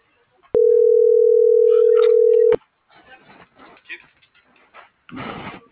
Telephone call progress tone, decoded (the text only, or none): ringback tone